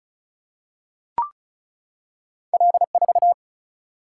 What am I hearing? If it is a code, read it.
Morse code, heard as a keyed tone, about 36 wpm, 700 Hz. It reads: L4
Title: Single-Letter Single-Number 35wpm